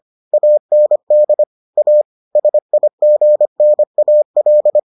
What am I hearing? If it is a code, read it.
Morse code, heard as a keyed tone, about 25 wpm, 600 Hz. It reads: AND A SIGNAL